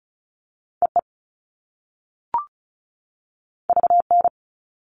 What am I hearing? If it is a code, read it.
Morse code, heard as a keyed tone, about 36 wpm, 690 Hz. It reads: EE VN
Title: All Cut Numbers - 2-Digit 35wpm